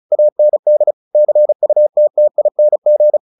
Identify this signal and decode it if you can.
Morse code, heard as a keyed tone, about 34 wpm, 600 Hz. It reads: AND CUTTING
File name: the old man and the sea 2 35wpm